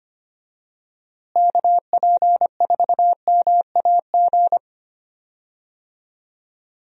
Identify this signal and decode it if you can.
Morse code, heard as a keyed tone, about 25 wpm, 700 Hz. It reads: KP4MAG
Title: USA Call Signs - 6 Characters - Repeated 3x - 25wpm